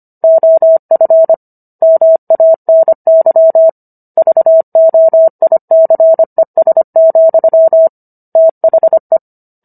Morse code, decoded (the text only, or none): OF MANY VOICES, THE